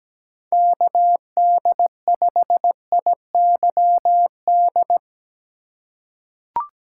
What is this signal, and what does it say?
Morse code, heard as a keyed tone, about 17 wpm, 700 Hz. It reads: KD5IYD
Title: USA Call Signs - 4x Character Spacing - 17wpm